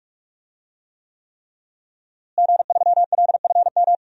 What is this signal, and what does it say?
Morse code, heard as a keyed tone, about 46 wpm, 700 Hz. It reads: G3LFM
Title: DXCC-100 Call Signs - 5 Characters - Repeated 3x - 45wpm